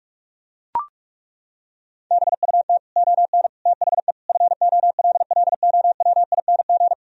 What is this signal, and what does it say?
Morse code, heard as a keyed tone, about 41 wpm, 700 Hz. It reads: BUT ON THE FOLLOWING
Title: Sentences from Top 1000 Words - Review Part 4 - 45wpm